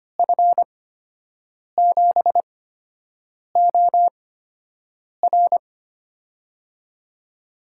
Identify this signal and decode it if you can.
Morse code, heard as a keyed tone, about 25 wpm, spaced as for 7 wpm, 700 Hz. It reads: F7OR
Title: USA Call Signs - 5 Characters - 8x Character Spacing - 25wpm